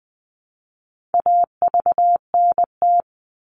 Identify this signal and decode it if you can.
Morse code, heard as a keyed tone, about 20 wpm, 700 Hz. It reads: AVNT
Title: All Cut Numbers - 4-Digit 20wpm